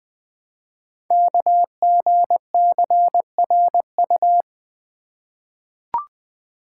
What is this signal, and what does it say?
Morse code, heard as a keyed tone, about 20 wpm, 700 Hz. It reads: KGCRU
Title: Code Groups - 5 Letters - 20wpm